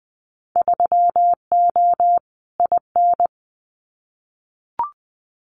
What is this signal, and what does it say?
Morse code, heard as a keyed tone, about 20 wpm, 700 Hz. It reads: 3O IN
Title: Winter Field Day Exchanges Encore 20wpm